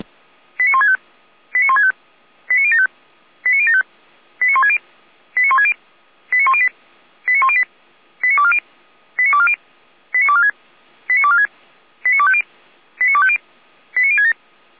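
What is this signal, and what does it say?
Five-tone selective call (ZVEI), heard as quick runs of five tones, tones of about 80 ms. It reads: 89165 89165 89075 89075 89160 89160 89108 89108 89240 89240 89246 89246 89250 89250 89067
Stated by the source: ZVEI five-tone call received by SDR, Feuerwehr Orlamünde.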